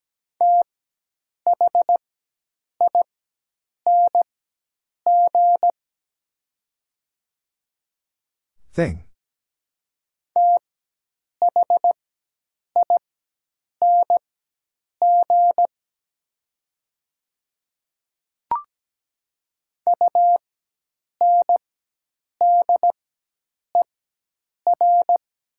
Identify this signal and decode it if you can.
Morse code, heard as a keyed tone, about 17 wpm, spaced as for 8 wpm, 700 Hz. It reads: THING THING UNDER
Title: 4x Character Spacing - Top 200 Words - 17wpm